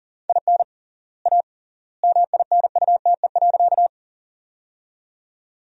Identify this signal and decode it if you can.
Morse code, heard as a keyed tone, about 40 wpm, 700 Hz. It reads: IN A MINUTE.
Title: 3x Word-Spacing - Sentences from Top 500 Words - 40wpm